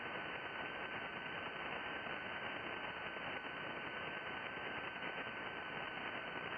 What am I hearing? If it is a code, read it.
FT8.
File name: contests / sept-vhf-k5tr- -2000